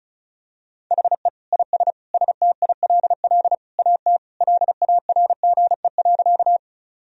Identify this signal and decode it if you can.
Morse code, heard as a keyed tone, about 34 wpm, 700 Hz. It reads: HE IS STILL AT LARGE.
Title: Sentences from Top 1000 Words - Review Part 1 - 35wpm